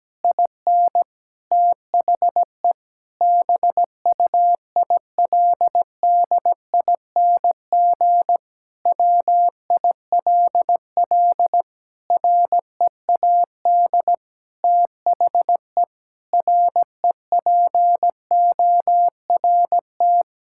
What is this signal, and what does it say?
Morse code, heard as a keyed tone, about 17 wpm, 700 Hz. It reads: IN THE BUILDING WILL READ THE REPORT
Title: Sentences from Top 1000 Words - Review Part 6 - 17wpm